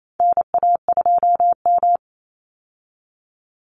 Morse code, heard as a keyed tone, about 28 wpm, 700 Hz. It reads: NA2M
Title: Call Signs Repeated 2x - 28wpm